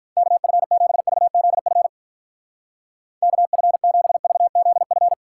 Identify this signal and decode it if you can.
Morse code, heard as a keyed tone, about 45 wpm, 700 Hz. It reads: KF7VZF KF7VZF